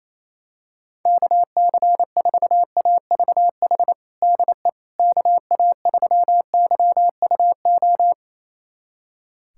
Morse code, heard as a keyed tone, about 28 wpm, 700 Hz. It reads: KC4AVH DE KA3YUO
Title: USA Callsign DE Callsign - 6 Characters - 28wpm